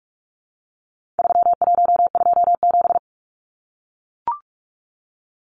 Morse code, heard as a keyed tone, about 44 wpm, 700 Hz. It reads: 3127
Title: Numbers - 4-Digit 45wpm